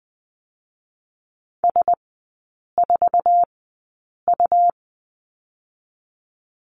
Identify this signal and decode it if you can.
Morse code, heard as a keyed tone, about 20 wpm, spaced as for 8 wpm, 700 Hz. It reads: S4U
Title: 100 Lesson 17e - Letter u - Three Characters 20wpm